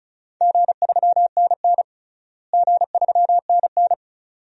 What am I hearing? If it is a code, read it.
Morse code, heard as a keyed tone, about 36 wpm, 700 Hz. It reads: G3NN G3NN
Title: DXCC-100 Call Signs - 4 Characters - Repeated 2x - 35wpm